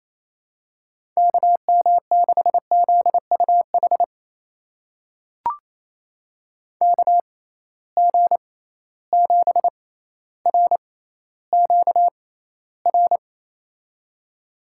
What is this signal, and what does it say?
Morse code, heard as a keyed tone, about 28 wpm, spaced as for 14 wpm, 700 Hz. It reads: KM6ZUH KG7RQR
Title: USA Call Signs - 6 Characters - 6x Character Spacing - 28wpm